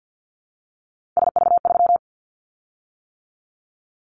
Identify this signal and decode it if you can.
Morse code, heard as a keyed tone, about 50 wpm, 700 Hz. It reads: SV3
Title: Callsign Prefixes for Top 50 DXCC - 50wpm